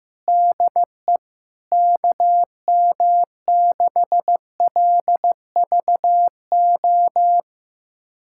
Morse code, heard as a keyed tone, about 15 wpm, 700 Hz. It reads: DE KM6LVO
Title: USA Callsign DE Callsign 15wpm